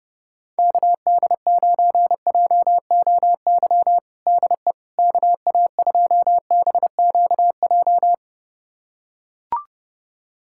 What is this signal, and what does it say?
Morse code, heard as a keyed tone, about 30 wpm, 700 Hz. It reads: KD9JOY DE KA2BQJ
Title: USA Callsign DE Callsign - 6 Characters - 30wpm